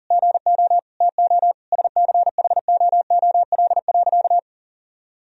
Morse code, heard as a keyed tone, about 40 wpm, 700 Hz. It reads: GO TO SCHOOL.
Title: Sentences from Top 1000 Words - Review Part 2 - 40wpm